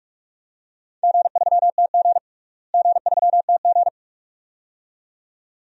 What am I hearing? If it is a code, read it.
Morse code, heard as a keyed tone, about 41 wpm, 700 Hz. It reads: G3TG G3TG
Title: DXCC-100 Call Signs - 4 Characters - Repeated 2x - 45wpm